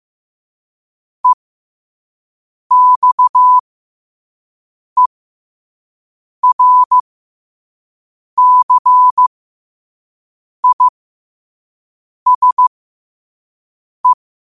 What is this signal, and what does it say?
Morse code, heard as a keyed tone, about 15 wpm, spaced as for 5 wpm, 1000 Hz. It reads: EXERCISE
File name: qotdfiles / 5wpm